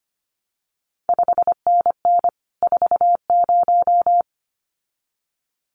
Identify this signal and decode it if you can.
Morse code, heard as a keyed tone, about 25 wpm, 700 Hz. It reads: 5NN 40
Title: CQ WW DX Contest Exchange Encore - 25wpm